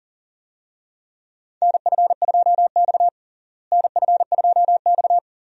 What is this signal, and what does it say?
Morse code, heard as a keyed tone, about 40 wpm, 700 Hz. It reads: NF2X NF2X
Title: USA Call Signs - 4 Characters - Repeated 2x - 40wpm